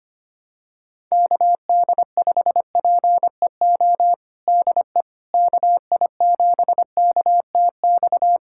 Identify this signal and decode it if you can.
Morse code, heard as a keyed tone, about 25 wpm, 700 Hz. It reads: KD5PEO DE KI7KTX
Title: USA Callsign DE Callsign - 6 Characters - 25wpm